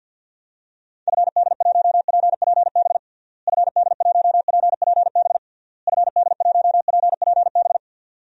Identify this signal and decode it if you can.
Morse code, heard as a keyed tone, about 51 wpm, 700 Hz. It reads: FD1PPB FD1PPB FD1PPB
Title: DXCC-100 Call Signs - 6 Characters - Repeated 3x - 50wpm